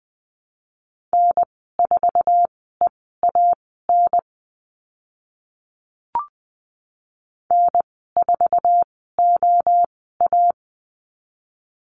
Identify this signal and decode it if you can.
Morse code, heard as a keyed tone, about 20 wpm, spaced as for 14 wpm, 700 Hz. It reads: N4EAN N4OA
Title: 062 Lesson 11g - Letter h - Call Signs 20wpm